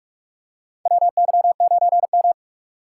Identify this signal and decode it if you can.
Morse code, heard as a keyed tone, about 43 wpm, 700 Hz. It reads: WY9M
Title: USA Call Signs - 6x Character Spacing - 45wpm